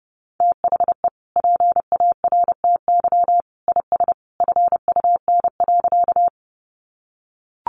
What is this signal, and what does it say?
Morse code, heard as a keyed tone, about 28 wpm, 700 Hz. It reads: THE PARTY IS FUN.
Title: Easy Phrases 30wpm